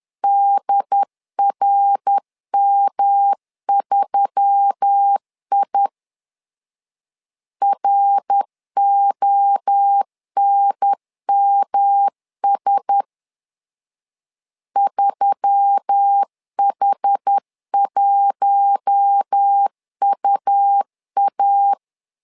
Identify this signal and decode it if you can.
Morse code, heard as a keyed tone, about 10 wpm, 800 Hz. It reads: DRM3I RONMS 3H1UA